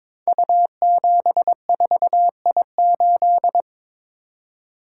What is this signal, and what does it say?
Morse code, heard as a keyed tone, about 23 wpm, 700 Hz. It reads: U74I8
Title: Code Groups - 5 Letter:Numbers - 22wpm